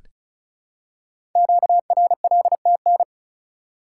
Morse code, heard as a keyed tone, about 36 wpm, 700 Hz. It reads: QRLTN
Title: Code Groups - 5 Letters - 35wpm